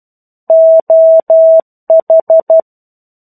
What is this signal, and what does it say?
Morse code, heard as a keyed tone, about 12 wpm, 650 Hz. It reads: OH
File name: 01-Chapter-01 12 WPM